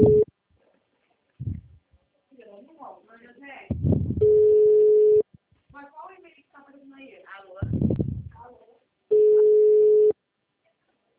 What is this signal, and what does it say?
Ringback tone (Europe, 425 Hz), with speech.